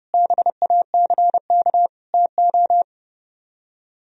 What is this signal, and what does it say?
Morse code, heard as a keyed tone, about 39 wpm, 700 Hz. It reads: BACK TO